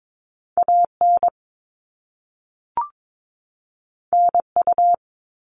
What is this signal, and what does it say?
Morse code, heard as a keyed tone, about 22 wpm, 700 Hz. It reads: AN NU
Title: All Cut Numbers - 2-Digit 22wpm